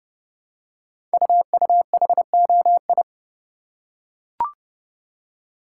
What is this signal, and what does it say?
Morse code, heard as a keyed tone, about 30 wpm, 700 Hz. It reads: UUHOI
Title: Code Groups - 5 Letters - 30wpm